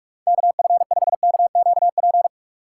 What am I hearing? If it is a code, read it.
Morse code, heard as a keyed tone, about 46 wpm, 700 Hz. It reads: KF5KQP
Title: USA Call Signs - 6 Characters - 2x Character Spacing - 45wpm